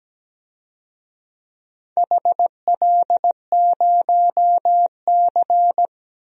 Morse code, heard as a keyed tone, about 17 wpm, 700 Hz. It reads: HL0C
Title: DXCC-100 Call Signs - 4 Characters - Repeated 3x - 17wpm